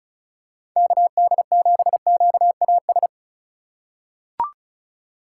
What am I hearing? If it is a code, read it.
Morse code, heard as a keyed tone, about 35 wpm, 700 Hz. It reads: KD7QAS
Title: USA Call Signs - 6 Characters - 3x Character Spacing - 35wpm